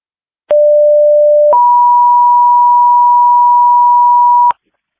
Two-tone page, heard as a one-second tone, then a three-second tone, about 590 then 970 Hz, with noise. It is heard from 0.5 to 4.5 s.